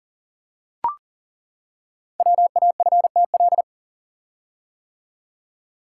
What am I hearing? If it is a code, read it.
Morse code, heard as a keyed tone, about 40 wpm, 700 Hz. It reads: WAFTL
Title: Code Groups - 5 Letters - 40wpm